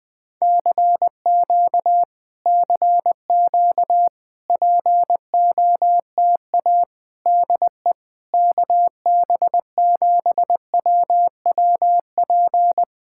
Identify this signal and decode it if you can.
Morse code, heard as a keyed tone, about 20 wpm, 700 Hz. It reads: CQ CQ POTA DE KB7WWP
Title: Speed-Racing Common QSO Phrases - 20wpm